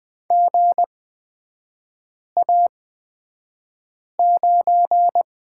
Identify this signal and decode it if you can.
Morse code, heard as a keyed tone, about 20 wpm, spaced as for 5 wpm, 700 Hz. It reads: GA9
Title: 255 Lesson 41e - Period - Three Characters 20wpm